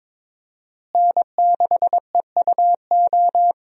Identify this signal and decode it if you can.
Morse code, heard as a keyed tone, about 22 wpm, 700 Hz. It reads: N6EUO
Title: USA Call Signs - 5 Characters - Repeated 3x - 22wpm